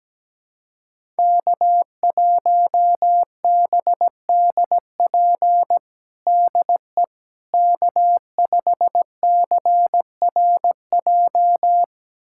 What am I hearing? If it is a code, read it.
Morse code, heard as a keyed tone, about 17 wpm, 700 Hz. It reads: K1BDP DE K5CRJ
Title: USA Callsign DE Callsign - 5 Characters - 17wpm